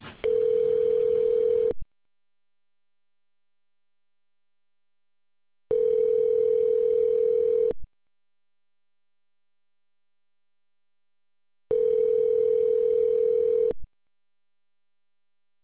Ringback tone (North America), 440 + 480 Hz.